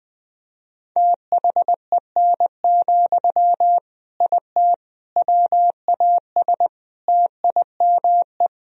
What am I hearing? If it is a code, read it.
Morse code, heard as a keyed tone, about 20 wpm, 700 Hz. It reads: THEN, IT WAS TIME
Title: Sentences from Top 1000 Words - Review Part 1 - 20wpm